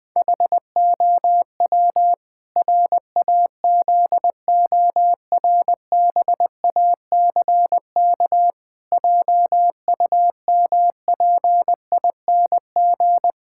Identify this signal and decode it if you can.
Morse code, heard as a keyed tone, about 20 wpm, 700 Hz. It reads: HOW RAZORBACK JUMPING
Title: Pangram Challenge v2 - 20wpm